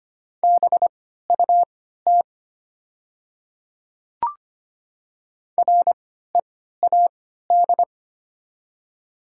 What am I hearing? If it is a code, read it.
Morse code, heard as a keyed tone, about 25 wpm, spaced as for 14 wpm, 700 Hz. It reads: BUT READ
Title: 3x Character Spacing - Top 300 Words - 25wpm